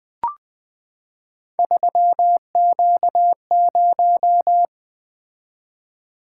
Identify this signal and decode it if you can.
Morse code, heard as a keyed tone, about 20 wpm, 700 Hz. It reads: 3Q0
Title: Three Letter-Number Combinations - 20wpm